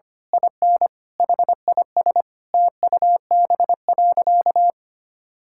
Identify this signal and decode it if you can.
Morse code, heard as a keyed tone, about 25 wpm, 700 Hz. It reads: IN HIS TUB.
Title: Pangram Challenge v2 - 25wpm